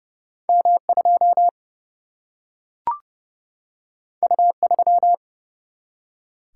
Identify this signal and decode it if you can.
Morse code, heard as a keyed tone, about 30 wpm, 700 Hz. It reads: M 2 U 3
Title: Single-Letter Single-Number 30wpm